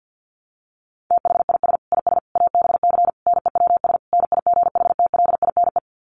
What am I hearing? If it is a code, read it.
Morse code, heard as a keyed tone, about 50 wpm, 700 Hz. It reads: THIS IS ABC NEWS NIGHTLINE
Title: Sets of 5 Words 50wpm